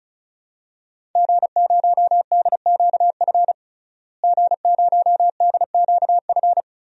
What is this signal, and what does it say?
Morse code, heard as a keyed tone, about 34 wpm, 700 Hz. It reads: G0DQF G0DQF
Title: DXCC-100 Call Signs - 5 Characters - Repeated 3x - 35wpm